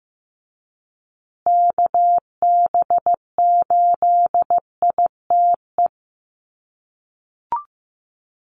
Morse code, heard as a keyed tone, about 15 wpm, 700 Hz. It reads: KB8ITE
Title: USA Call Signs Repeated 2x 15wpm